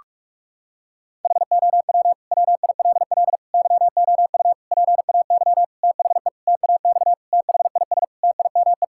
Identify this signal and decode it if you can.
Morse code, heard as a keyed tone, about 46 wpm, 700 Hz. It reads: HOW WILL YOU PAY THE TAX THIS TIME
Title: Sentences from Top 1000 Words - Review Part 5 - 45wpm